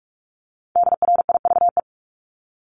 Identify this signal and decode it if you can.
Morse code, heard as a keyed tone, about 46 wpm, 700 Hz. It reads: DRIVE
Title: Multi-letter Phonemes Repeated 3x 45wpm